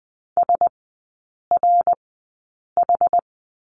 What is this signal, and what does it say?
Morse code, heard as a keyed tone, about 20 wpm, spaced as for 8 wpm, 690 Hz. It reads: SRH
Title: 081 Lesson 14e - Number 2 - Three Characters 20wpm